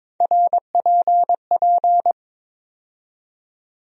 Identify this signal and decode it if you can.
Morse code, heard as a keyed tone, about 22 wpm, 700 Hz. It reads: RPP
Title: USA Call Sign Postfixes - 3 Letters 22wpm